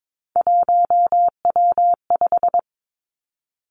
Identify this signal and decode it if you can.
Morse code, heard as a keyed tone, about 22 wpm, 700 Hz. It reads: 1W5
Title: Three Challenging Letter-Number Combinations 22wpm